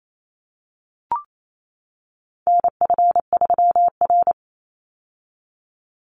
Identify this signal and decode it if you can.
Morse code, heard as a keyed tone, about 28 wpm, 700 Hz. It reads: NF3R
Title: Call Signs - 28wpm